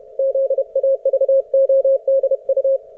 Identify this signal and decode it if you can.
Morse code, heard as a keyed tone, about 31 wpm, 540 Hz. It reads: ZAVODU